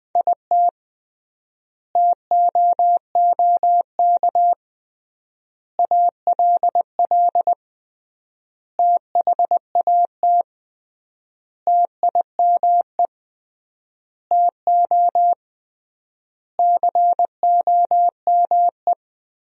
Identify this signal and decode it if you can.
Morse code, heard as a keyed tone, about 21 wpm, 700 Hz. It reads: IT TOOK ALL THAT TIME TO COME
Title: 3x Word-Spacing - Sentences from Top 300 Words - 20wpm